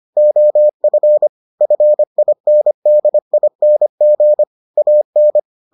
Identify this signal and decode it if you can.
Morse code, heard as a keyed tone, about 25 wpm, 600 Hz. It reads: OF FINDING AN